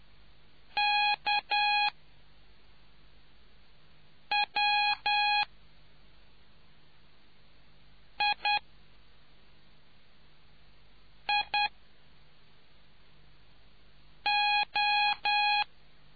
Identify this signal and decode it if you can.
Morse code, heard as a keyed tone, about 10 wpm, spaced as for 3 wpm, 830 Hz. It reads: KWIIO